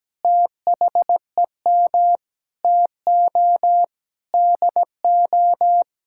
Morse code, heard as a keyed tone, about 17 wpm, 700 Hz. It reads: THEM TO DO
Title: Sentences from Top 1000 Words - Review Part 5 - 17wpm